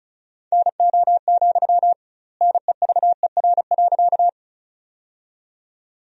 Morse code, heard as a keyed tone, about 34 wpm, 700 Hz. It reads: NO, NEVER.
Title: 2x Word-Spacing - Sentences from Top 300 Words - 35wpm